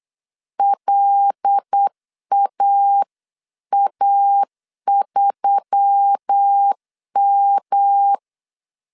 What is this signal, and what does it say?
Morse code, heard as a keyed tone, about 8 wpm, 800 Hz. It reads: LAA3M
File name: 8wpm